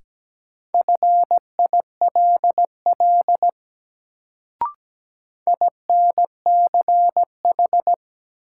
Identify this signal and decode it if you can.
Morse code, heard as a keyed tone, about 17 wpm, 700 Hz. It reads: FILL INCH
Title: Sentences from Top 1000 Words - Review Part 3 - New Word Drill - 17wpm